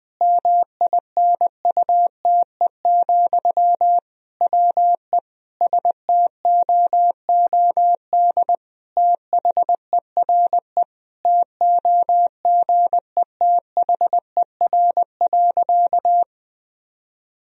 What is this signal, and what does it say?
Morse code, heard as a keyed tone, about 20 wpm, 700 Hz. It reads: MINUTE, WE STOOD THERE TOGETHER.
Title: Sentences from Top 1000 Words - Review Part 2 - 20wpm